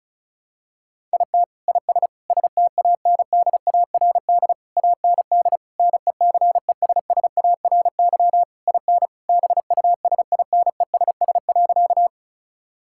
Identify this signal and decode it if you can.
Morse code, heard as a keyed tone, about 34 wpm, 700 Hz. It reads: IT IS STANDARD AND NECESSARY IN BUSINESS.
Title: Sentences from Top 1000 Words - Review Part 6 - 35wpm